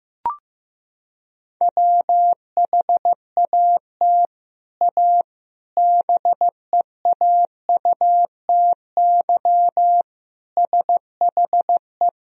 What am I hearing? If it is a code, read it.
Morse code, heard as a keyed tone, about 15 wpm, 700 Hz. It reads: WHAT A BEAUTY SHE
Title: Sentences from Top 1000 Words - Review Part 2 - 15wpm